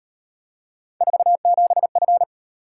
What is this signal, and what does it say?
Morse code, heard as a keyed tone, about 37 wpm, 700 Hz. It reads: 47F